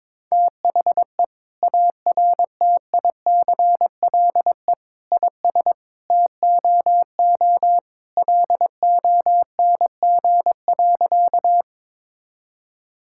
Morse code, heard as a keyed tone, about 21 wpm, 700 Hz. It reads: THE ARTICLE IS TOO LONG.
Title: Sentences from Top 1000 Words - Review Part 5 - 22wpm